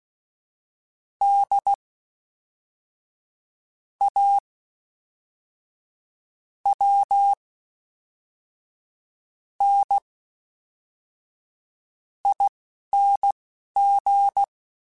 Morse code, heard as a keyed tone, about 16 wpm, spaced as for 5 wpm, 800 Hz. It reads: DAWNING
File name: Special INGs / Special INGs-mp3s / Fast ing @10